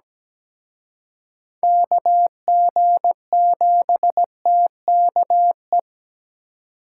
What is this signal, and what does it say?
Morse code, heard as a keyed tone, about 20 wpm, 700 Hz. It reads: KG7TKE
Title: Speed Racing - USA 2x3 Call Signs 17wpm